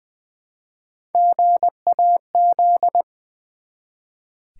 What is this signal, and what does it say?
Morse code, heard as a keyed tone, about 20 wpm, 690 Hz. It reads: GAZ
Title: Build A Word - 1 Letter At A Time - 4-Letter Words - 20wpm